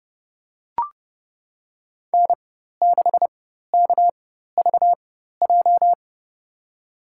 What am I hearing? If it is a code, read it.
Morse code, heard as a keyed tone, about 30 wpm, spaced as for 14 wpm, 700 Hz. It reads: N6KVJ
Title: USA Call Signs - 5 Characters - 4x Character Spacing - 30wpm